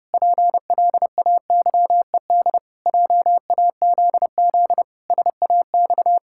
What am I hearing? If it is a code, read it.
Morse code, heard as a keyed tone, about 30 wpm, 700 Hz. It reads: PLAYED JAZZ SAX
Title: Pangram Challenge v2 - 30wpm